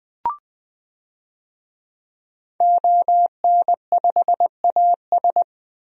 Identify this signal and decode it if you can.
Morse code, heard as a keyed tone, about 20 wpm, 700 Hz. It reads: ON5AS
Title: DXCC-100 Call Signs - 5 Characters - 20wpm